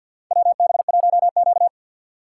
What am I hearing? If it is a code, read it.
Morse code, heard as a keyed tone, about 53 wpm, 700 Hz. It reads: WB1Q
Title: USA Call Signs - 4 Characters - 6x Character Spacing - 50wpm